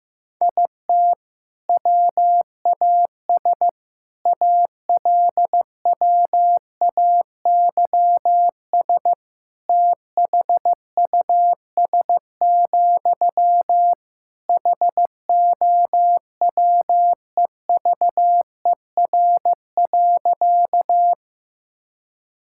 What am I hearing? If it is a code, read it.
Morse code, heard as a keyed tone, about 16 wpm, 700 Hz. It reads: IT WAS ALWAYS THUS, HOWEVER.
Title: Sentences from Top 1000 Words - Review Part 5 - 15wpm